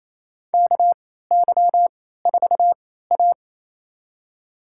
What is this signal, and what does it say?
Morse code, heard as a keyed tone, about 27 wpm, spaced as for 16 wpm, 700 Hz. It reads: KY4A
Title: USA Call Signs - 3x Character Spacing - 28wpm